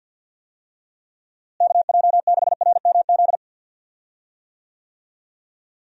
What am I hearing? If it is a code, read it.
Morse code, heard as a keyed tone, about 51 wpm, 700 Hz. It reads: KJ6RMZ
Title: USA Call Signs - 6 Characters - Repeated 3x - 50wpm